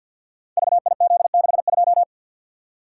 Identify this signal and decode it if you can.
Morse code, heard as a keyed tone, about 52 wpm, 700 Hz. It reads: VIZ62
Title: Code Groups - 5 Letter:Numbers - 50wpm